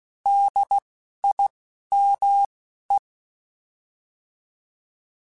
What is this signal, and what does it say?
Morse code, heard as a keyed tone, about 16 wpm, spaced as for 12 wpm, 800 Hz. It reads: DIME
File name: Alphabetical / Alphabetical mp3s / Ds @10